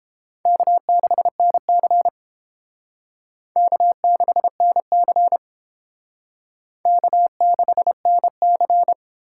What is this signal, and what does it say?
Morse code, heard as a keyed tone, about 30 wpm, 700 Hz. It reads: K6NC K6NC K6NC